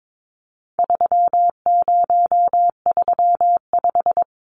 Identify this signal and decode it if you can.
Morse code, heard as a keyed tone, about 22 wpm, 700 Hz. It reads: 3035
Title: Numbers - 4-Digit 22wpm